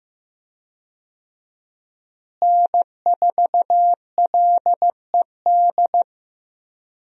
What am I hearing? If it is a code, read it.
Morse code, heard as a keyed tone, about 15 wpm, 700 Hz. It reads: N4LED
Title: USA Call Signs - 5 Characters - Repeated 2x - 15wpm